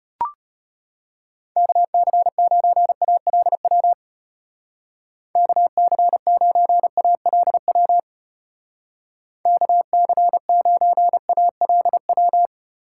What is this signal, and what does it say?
Morse code, heard as a keyed tone, about 30 wpm, 700 Hz. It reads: KC9ALW KC9ALW KC9ALW
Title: Speed Racing - USA Call Signs 25wpm